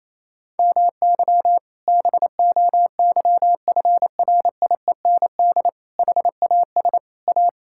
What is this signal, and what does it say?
Morse code, heard as a keyed tone, about 28 wpm, 700 Hz. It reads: MY BOYFRIEND HAS A
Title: Sentences from Top 1000 Words - Review Part 5 - 28wpm